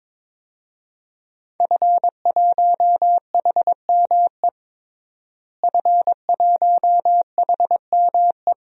Morse code, heard as a keyed tone, about 22 wpm, 700 Hz. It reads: F1HME F1HME
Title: DXCC-100 Call Signs - 5 Characters - Repeated 3x - 22wpm